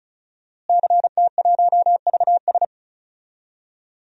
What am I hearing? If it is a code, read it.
Morse code, heard as a keyed tone, about 36 wpm, 700 Hz. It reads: CT1VS
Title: DXCC-100 Call Signs - 5 Characters - Repeated 2x - 35wpm